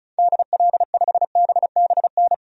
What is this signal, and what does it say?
Morse code, heard as a keyed tone, about 36 wpm, 690 Hz. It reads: DL5BBN
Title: DXCC-100 Call Signs - 6 Characters - 35wpm